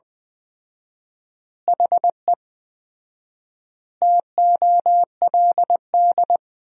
Morse code, heard as a keyed tone, about 20 wpm, 700 Hz. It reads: HE TOLD
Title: 4x Word-Spacing - Sentences from Top 400 Words - 20wpm